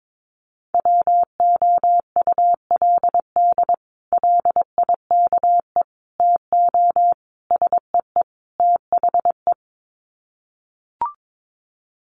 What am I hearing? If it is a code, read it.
Morse code, heard as a keyed tone, about 22 wpm, 700 Hz. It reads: WOULD LIKE TO SEE THE
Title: Sets of 5 Words Encore2 22wpm